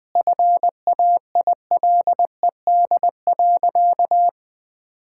Morse code, heard as a keyed tone, about 20 wpm, 700 Hz. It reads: FAILED.
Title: Sentences with 9-Word Phrases - 20wpm